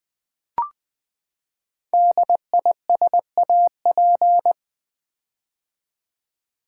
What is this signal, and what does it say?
Morse code, heard as a keyed tone, about 20 wpm, 700 Hz. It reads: DISAP
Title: Build A Word - 5 Letters At A Time - 12-Letter Words - 20wpm